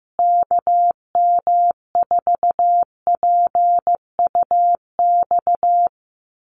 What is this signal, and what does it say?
Morse code, heard as a keyed tone, about 15 wpm, 700 Hz. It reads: KM4PUX
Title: USA Call Signs Repeated 3x 15wpm